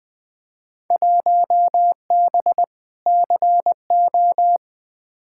Morse code, heard as a keyed tone, about 20 wpm, 700 Hz. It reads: 1B CO
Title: Field Day Exchanges - 20wpm